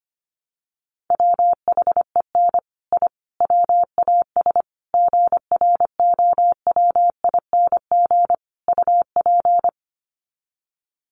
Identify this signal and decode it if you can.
Morse code, heard as a keyed tone, about 25 wpm, 700 Hz. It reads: WHEN I WAS GROWING UP
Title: Sets of 5 Words Encore 25wpm